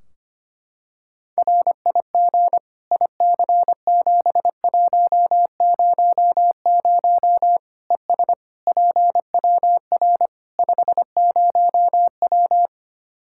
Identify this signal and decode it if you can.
Morse code, heard as a keyed tone, about 25 wpm, 700 Hz. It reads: RIG IC7100 ES PWR 50W
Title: Practice QSO - Exchange 2 - Info - 25wpm